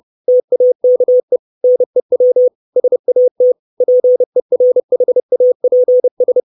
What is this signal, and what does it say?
Morse code, heard as a keyed tone, about 30 wpm, 500 Hz. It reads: TAKE NEW SAT PERHAPS
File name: 632 Words 30WPM